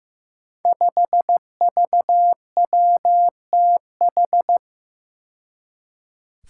Morse code, heard as a keyed tone, about 15 wpm, 700 Hz. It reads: 5VWTH
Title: Five Challenging Letter-Number Combinations 15wpm